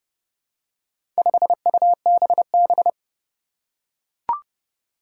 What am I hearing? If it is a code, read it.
Morse code, heard as a keyed tone, about 30 wpm, 700 Hz. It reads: 5UBB
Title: Four Challenging Letter-Number Combinations 30wpm